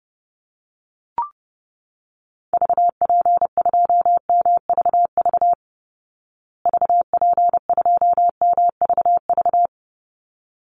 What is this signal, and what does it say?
Morse code, heard as a keyed tone, about 30 wpm, 700 Hz. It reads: VP2MVV VP2MVV
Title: Call Signs Repeated 3x - 30wpm